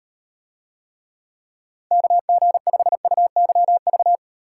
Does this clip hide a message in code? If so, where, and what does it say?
Yes: Morse code from 1.9 to 4.2 s, heard as a keyed tone, about 38 wpm, 700 Hz. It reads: KG5UYV